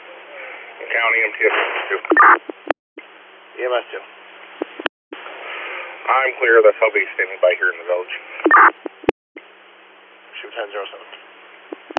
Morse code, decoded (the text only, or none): none